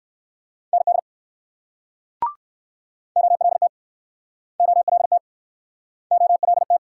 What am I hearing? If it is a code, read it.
Morse code, heard as a keyed tone, about 56 wpm, 690 Hz. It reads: ND OLT OLT OLT
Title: Speed-Racing Multi-letter Phonemes - 45wpm